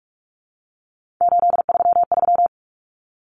Morse code, heard as a keyed tone, about 44 wpm, 700 Hz. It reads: 833
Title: Common Cut Numbers - 3-Digit 45wpm